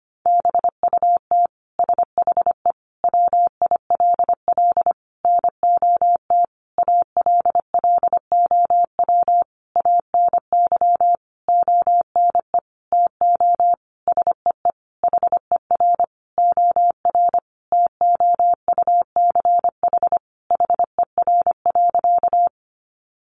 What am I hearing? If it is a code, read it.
Morse code, heard as a keyed tone, about 25 wpm, 700 Hz. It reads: BUT SHE WILL NOT ALLOW ANY ONE TO SEE HER OR TOUCH HER.
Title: Sentences from Top 1200 Words 25wpm